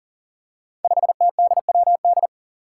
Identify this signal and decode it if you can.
Morse code, heard as a keyed tone, about 40 wpm, 700 Hz. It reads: 5TDWD
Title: Five Challenging Letter-Number Combinations 40wpm